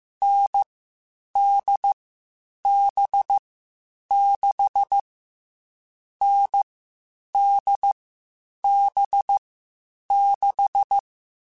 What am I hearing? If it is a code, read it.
Morse code, heard as a keyed tone, about 15 wpm, 800 Hz. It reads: N D B 6 N D B 6
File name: Short Runs / Short Runs-mp3s / NDB NDB6 @13